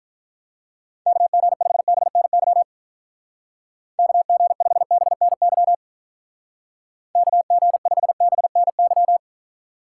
Morse code, heard as a keyed tone, about 45 wpm, 700 Hz. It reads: KG5BNY KG5BNY KG5BNY